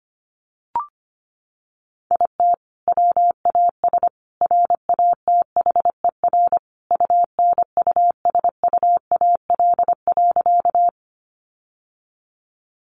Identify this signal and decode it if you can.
Morse code, heard as a keyed tone, about 25 wpm, 700 Hz. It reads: IT WAS RATHER UNUSUAL.
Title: Sentences from Top 1400 Words 25wpm